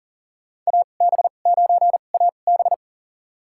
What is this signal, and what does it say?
Morse code, heard as a keyed tone, about 40 wpm, spaced as for 29 wpm, 700 Hz. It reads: AB9AB
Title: USA Call Signs - 5 Characters - 2x Character Spacing - 40wpm